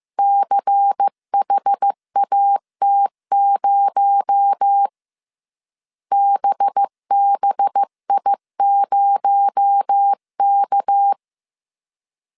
Morse code, heard as a keyed tone, about 15 wpm, 800 Hz. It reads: CHAT0 BBI0K